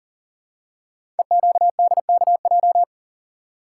Morse code, heard as a keyed tone, about 42 wpm, 700 Hz. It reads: EQDKJ
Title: Code Groups - 5 Letters - 40wpm